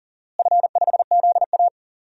Morse code, heard as a keyed tone, about 40 wpm, 700 Hz. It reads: F5ZA
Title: DXCC-100 Call Signs - 4 Characters - Repeated 2x - 40wpm